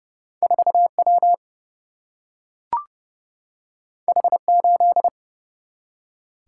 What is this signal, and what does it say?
Morse code, heard as a keyed tone, about 30 wpm, 700 Hz. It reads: 4 W H 8
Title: Single-Letter Single-Number 30wpm